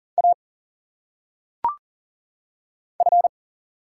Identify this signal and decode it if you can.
Morse code, heard as a keyed tone, about 40 wpm, 700 Hz. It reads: A F
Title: Single Letters - Phonetic Pronunciation 40wpm